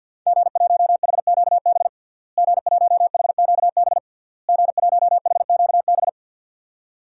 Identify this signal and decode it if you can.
Morse code, heard as a keyed tone, about 52 wpm, 700 Hz. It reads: G1HQB G1HQB G1HQB
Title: DXCC-100 Call Signs - 5 Characters - Repeated 3x - 50wpm